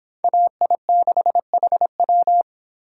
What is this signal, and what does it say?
Morse code, heard as a keyed tone, about 25 wpm, 700 Hz. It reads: AI6HW